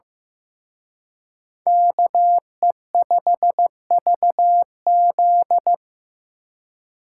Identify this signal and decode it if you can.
Morse code, heard as a keyed tone, about 18 wpm, 700 Hz. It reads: KE5VZ
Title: Speed Racing - USA Call Signs 15wpm